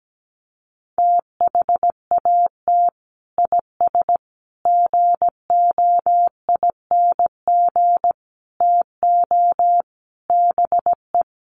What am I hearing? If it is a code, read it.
Morse code, heard as a keyed tone, about 17 wpm, 700 Hz. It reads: THAT IS GOING TO BE
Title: Sets of 5 Words Encore2 17wpm